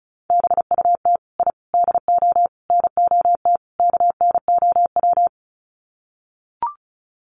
Morse code, heard as a keyed tone, about 34 wpm, 700 Hz. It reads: BUT I DO NOT KNOW
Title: Sets of 5 Words 35wpm